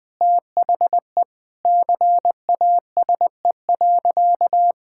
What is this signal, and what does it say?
Morse code, heard as a keyed tone, about 19 wpm, 700 Hz. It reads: THE CASE.
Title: Sentences from Top 1000 Words - Review Part 6 - 20wpm